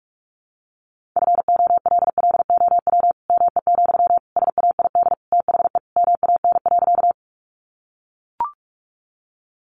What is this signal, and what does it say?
Morse code, heard as a keyed tone, about 44 wpm, 700 Hz. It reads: FOLLOW ME, SAID THE MAN.
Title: Sentences from Top 200 Words 45wpm